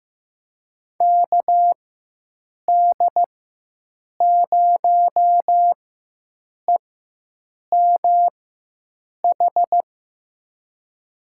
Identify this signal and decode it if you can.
Morse code, heard as a keyed tone, about 15 wpm, spaced as for 7 wpm, 700 Hz. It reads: KD0EMH
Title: USA Call Signs - 6 Characters - 4x Character Spacing - 15wpm